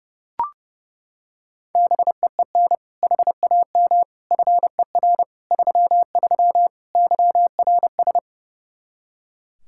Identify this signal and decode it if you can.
Morse code, heard as a keyed tone, about 30 wpm, 700 Hz. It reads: BEEN HAM FER 33 YRS
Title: Practice QSO - Exchange 3 - More Info - 30wpm